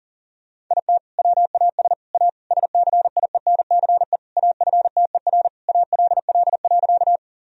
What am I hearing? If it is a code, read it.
Morse code, heard as a keyed tone, about 37 wpm, 700 Hz. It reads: IT WAS A SCIENCE AFTER ALL.
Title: Sentences from Top 1000 Words - Review Part 2 - 40wpm